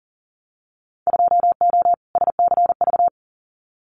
Morse code, heard as a keyed tone, about 40 wpm, 700 Hz. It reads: 2O SCV
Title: Winter Field Day Exchanges Encore 40wpm